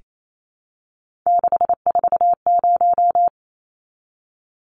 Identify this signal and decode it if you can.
Morse code, heard as a keyed tone, about 28 wpm, 700 Hz. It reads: 640
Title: Numbers - 3-Digit 28wpm